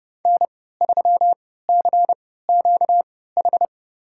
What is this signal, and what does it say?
Morse code, heard as a keyed tone, about 31 wpm, spaced as for 17 wpm, 700 Hz. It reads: N3CQH
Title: USA Call Signs - 3x Character Spacing - 30wpm